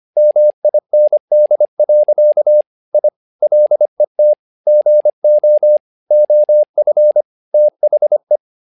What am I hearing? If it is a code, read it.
Morse code, heard as a keyed tone, about 25 wpm, 600 Hz. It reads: MIND. I LET GO OF THE
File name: learning center / tower trouble / tower trouble 25wpm